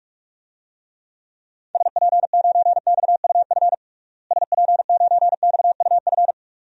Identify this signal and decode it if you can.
Morse code, heard as a keyed tone, about 44 wpm, 700 Hz. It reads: SP9XUF SP9XUF
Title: DXCC-100 Call Signs - 6 Characters - Repeated 2x - 45wpm